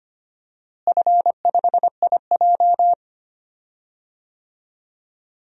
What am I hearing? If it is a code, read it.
Morse code, heard as a keyed tone, about 25 wpm, 700 Hz. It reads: F5IJ
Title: DXCC-100 Call Signs - 4 Characters - Repeated 2x - 25wpm